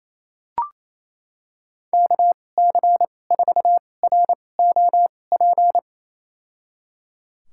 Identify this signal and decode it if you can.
Morse code, heard as a keyed tone, about 28 wpm, spaced as for 20 wpm, 700 Hz. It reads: KC4ROP
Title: USA Call Signs - 6 Characters - 2x Character Spacing - 28wpm